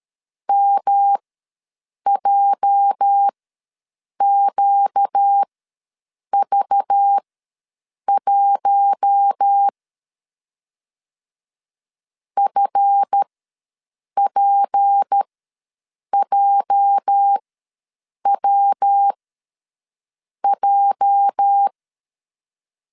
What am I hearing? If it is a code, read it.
Morse code, heard as a keyed tone, about 13 wpm, spaced as for 7 wpm, 800 Hz. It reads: MJQV1 FPJWJ
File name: LEZIONI / 8 YJ45 / 8 YJ ws 12wpm 3342